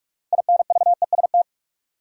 Morse code, heard as a keyed tone, about 43 wpm, 700 Hz. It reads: INVEST
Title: Sentences from Top 1100 Words - New Word Drill - 45wpm